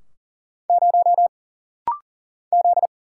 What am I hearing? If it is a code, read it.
Morse code, heard as a keyed tone, about 40 wpm, 700 Hz. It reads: 0 Z
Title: Single Letter-Number Alt - Rapid-Fire 40wpm